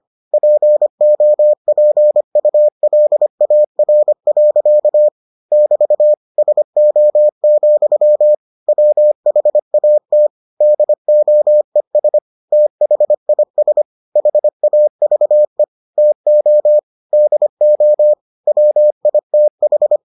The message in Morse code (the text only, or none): POPULAR. <BT> SO, WHAT DOES THIS HAVE TO DO WITH